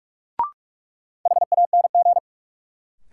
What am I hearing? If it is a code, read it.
Morse code, heard as a keyed tone, about 44 wpm, 700 Hz. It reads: HANG
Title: Sentences from Top 1600 Words - New Word Drill - 45wpm